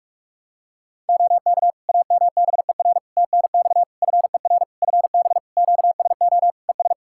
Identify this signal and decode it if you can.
Morse code, heard as a keyed tone, about 46 wpm, 700 Hz. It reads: OK AMBER TNX FER FB QSO ES
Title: Practice QSO - Exchange 4 - Ending - 45wpm